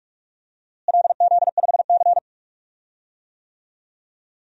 Morse code, heard as a keyed tone, about 41 wpm, 700 Hz. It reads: LZ5C
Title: DXCC-100 Call Signs - 4 Characters - Repeated 3x - 45wpm